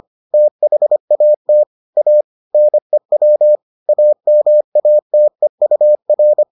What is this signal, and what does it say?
Morse code, heard as a keyed tone, about 25 wpm, 600 Hz. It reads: THAT A NEW AMATEUR